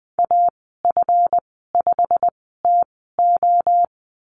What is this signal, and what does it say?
Morse code, heard as a keyed tone, about 20 wpm, spaced as for 14 wpm, 700 Hz. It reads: AF5TO
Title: 140 Lesson 23g - Letter f - Call Signs 20wpm